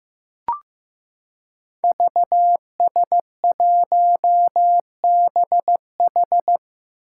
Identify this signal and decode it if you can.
Morse code, heard as a keyed tone, about 15 wpm, 700 Hz. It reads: VS1BH
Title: Five Challenging Letter-Number Combinations 15wpm